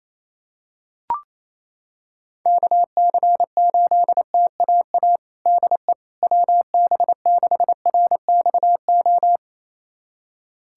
Morse code, heard as a keyed tone, about 28 wpm, 700 Hz. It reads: KC8TAA DE WB6RXO
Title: USA Callsign DE Callsign - 6 Characters - 28wpm